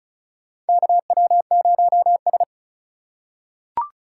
Morse code, heard as a keyed tone, about 34 wpm, 700 Hz. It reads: KW0S
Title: USA Call Signs - 4 Characters - 3x Character Spacing - 35wpm